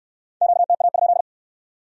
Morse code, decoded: QTA?

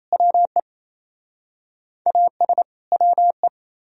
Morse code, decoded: WE AS WE

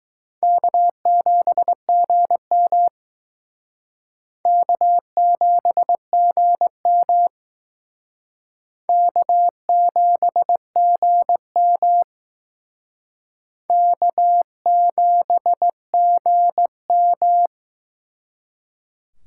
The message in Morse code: K7GM K7GM K7GM K7GM

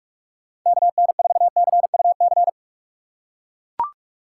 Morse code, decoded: KN4CUC

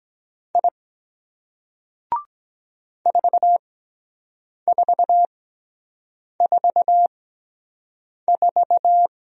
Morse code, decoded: I 4444

